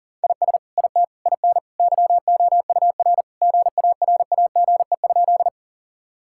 Morse code, heard as a keyed tone, about 40 wpm, 700 Hz. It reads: IS IT IN YOUR GARAGE?
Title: Sentences from Top 1600 Words 40wpm